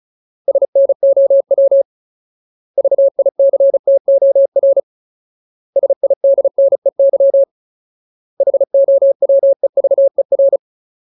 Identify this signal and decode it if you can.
Morse code, heard as a keyed tone, about 34 wpm, 550 Hz. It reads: SNOW VICTOR SIDNEY HOWEVER
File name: Ultimate Word List / 35WPM Ultimate